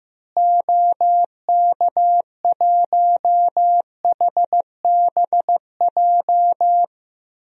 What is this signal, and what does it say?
Morse code, heard as a keyed tone, about 15 wpm, 700 Hz. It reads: OK1HBJ
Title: DXCC-100 Call Signs - 6 Characters - Repeated 3x - 15wpm